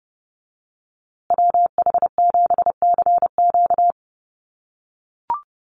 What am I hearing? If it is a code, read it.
Morse code, heard as a keyed tone, about 28 wpm, 700 Hz. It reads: WH7CQ
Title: USA Call Signs Repeated 2x 30wpm